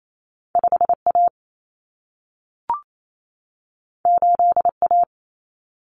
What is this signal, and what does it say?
Morse code, heard as a keyed tone, about 28 wpm, 700 Hz. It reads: 5A 8A
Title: Common Cut Numbers - 2-Digit 28wpm